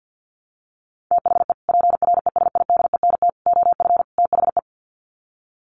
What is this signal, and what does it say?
Morse code, heard as a keyed tone, about 50 wpm, 700 Hz. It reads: THE PRESIDENT OF THE
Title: Sets of 4 Words 50wpm